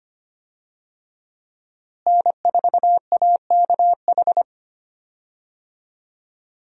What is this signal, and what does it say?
Morse code, heard as a keyed tone, about 25 wpm, 700 Hz. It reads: N4AKH
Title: USA Call Signs - 5 Characters - Repeated 3x - 25wpm